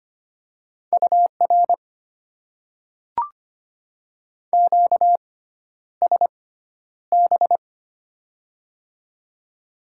Morse code, heard as a keyed tone, about 25 wpm, spaced as for 11 wpm, 700 Hz. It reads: UR QSB
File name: Top 90 QSO Elements - 6x Character Spacing - 25wpm